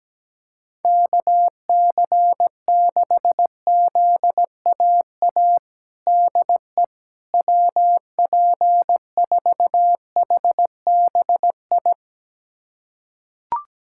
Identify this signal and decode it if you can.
Morse code, heard as a keyed tone, about 17 wpm, 700 Hz. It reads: KC6ZAA DE WP4HBI
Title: USA Callsign DE Callsign - 6 Characters - 17wpm